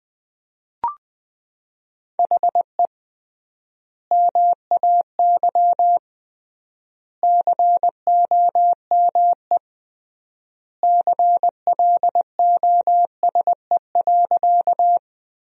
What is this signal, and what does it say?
Morse code, heard as a keyed tone, about 20 wpm, 700 Hz. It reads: HE MAY COME CLOSE.
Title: 3x Word-Spacing - Sentences from Top 300 Words - 20wpm